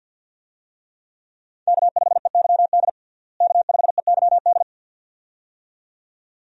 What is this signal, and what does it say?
Morse code, heard as a keyed tone, about 52 wpm, 700 Hz. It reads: K5EYD K5EYD
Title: USA Call Signs - 5 Characters - Repeated 2x - 50wpm